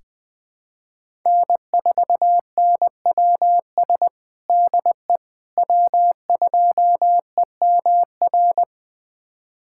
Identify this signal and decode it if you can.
Morse code, heard as a keyed tone, about 20 wpm, 700 Hz. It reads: N4NWS DE W2EMR
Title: USA Callsign DE Callsign - 5 Characters - 20wpm